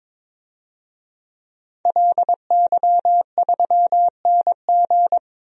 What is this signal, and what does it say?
Morse code, heard as a keyed tone, about 21 wpm, 700 Hz. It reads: LY3NG
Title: DXCC-100 Call Signs - 5 Characters - Repeated 2x - 22wpm